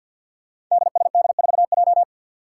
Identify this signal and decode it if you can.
Morse code, heard as a keyed tone, about 51 wpm, 700 Hz. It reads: DSD4J
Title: Five Challenging Letter-Number Combinations 50wpm